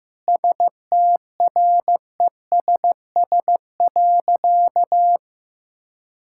Morse code, heard as a keyed tone, about 15 wpm, 700 Hz. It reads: STRESS.
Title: Sentences from Top 1000 Words - Review Part 6 - 15wpm